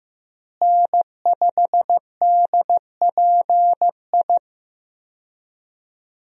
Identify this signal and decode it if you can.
Morse code, heard as a keyed tone, about 15 wpm, 700 Hz. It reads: N5DPI
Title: USA Call Signs - 5 Characters - Repeated 3x - 15wpm